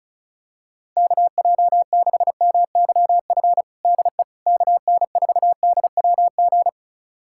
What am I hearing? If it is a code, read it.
Morse code, heard as a keyed tone, about 34 wpm, 700 Hz. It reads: KJ6MYF DE KN4DWG
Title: USA Callsign DE Callsign 35wpm